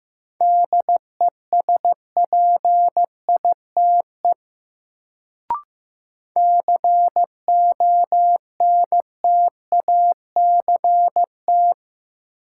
Morse code, heard as a keyed tone, about 15 wpm, 700 Hz. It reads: DESPITE CONTACT
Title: Sentences from Top 1000 Words - Review Part 7 - New Word Drill - 15wpm